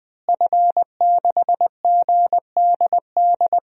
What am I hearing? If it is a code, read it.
Morse code, heard as a keyed tone, about 20 wpm, 700 Hz. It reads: F6GDD
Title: DXCC-100 Call Signs - 5 Characters - Repeated 3x - 20wpm